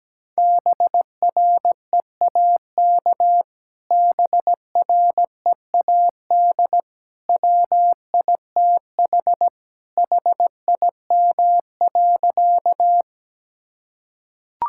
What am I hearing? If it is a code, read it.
Morse code, heard as a keyed tone, about 17 wpm, 700 Hz. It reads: BREAK BREAD WITH HIM.
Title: Sentences from Top 1600 Words 17wpm